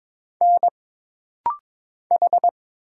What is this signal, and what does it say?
Morse code, heard as a keyed tone, about 22 wpm, 700 Hz. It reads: N H
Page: This rapid-fire timing is geared to push your ICR (Instant Character Recognition) skills.